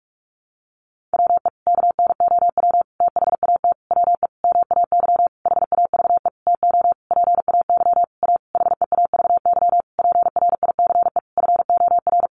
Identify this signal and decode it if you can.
Morse code, heard as a keyed tone, about 44 wpm, 700 Hz. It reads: WE KNOW THAT WE MAY HAVE TO PAY A HEAVY PRICE FOR
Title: Sentences from Top 1100 Words 45wpm